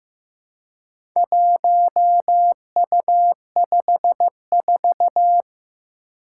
Morse code, heard as a keyed tone, about 15 wpm, 700 Hz. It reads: 1U54
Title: Four Challenging Letter-Number Combinations 15wpm